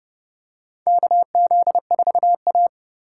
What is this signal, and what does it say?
Morse code, heard as a keyed tone, about 30 wpm, 700 Hz. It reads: KZ4A
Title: K1USN SST QSO Practice - 30wpm